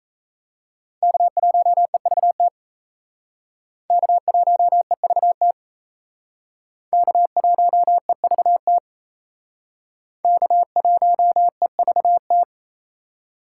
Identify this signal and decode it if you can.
Morse code, heard as a keyed tone, about 35 wpm, 700 Hz. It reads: K1EVT K1EVT K1EVT K1EVT